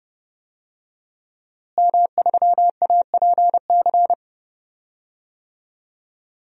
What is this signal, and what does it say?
Morse code, heard as a keyed tone, about 30 wpm, 700 Hz. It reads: M3APC
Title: DXCC-100 Call Signs - 5 Characters - Repeated 3x - 30wpm